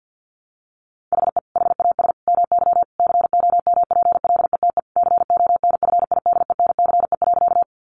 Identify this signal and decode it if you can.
Morse code, heard as a keyed tone, about 50 wpm, 700 Hz. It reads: HE HAS MY COMPLETE CONFIDENCE.
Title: Sentences from Top 1500 Words 50wpm